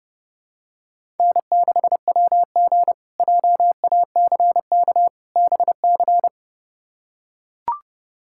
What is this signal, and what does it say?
Morse code, heard as a keyed tone, about 30 wpm, 700 Hz. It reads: N6WG JACK BC
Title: K1USN SST QSO Practice - 30wpm